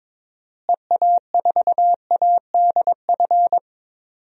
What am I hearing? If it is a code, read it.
Morse code, heard as a keyed tone, about 21 wpm, 700 Hz. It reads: EA4ADF
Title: DXCC-100 Call Signs - 6 Characters - Repeated 2x - 22wpm